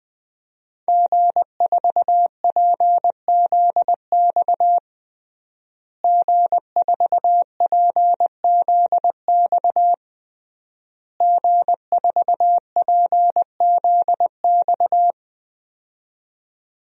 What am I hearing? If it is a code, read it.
Morse code, heard as a keyed tone, about 20 wpm, 700 Hz. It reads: G4PZX G4PZX G4PZX
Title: DXCC-100 Call Signs - 5 Characters - Repeated 3x - 20wpm